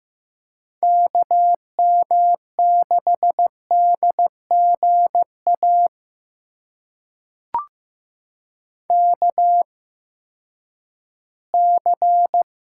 Morse code, heard as a keyed tone, about 15 wpm, spaced as for 10 wpm, 700 Hz. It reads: KM6DGA KC
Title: USA Call Signs - 6 Characters - 8x Character Spacing - 15wpm